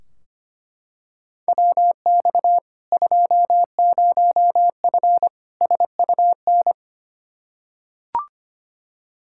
Morse code, heard as a keyed tone, about 25 wpm, 700 Hz. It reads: WX 20F SUN
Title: Practice QSO - Exchange 2 - Info - 25wpm